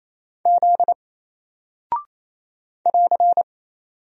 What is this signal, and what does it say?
Morse code, heard as a keyed tone, about 27 wpm, 700 Hz. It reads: Z <AR>
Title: Single Character (Letter:Number:Punctuation:Prosign) 28wpm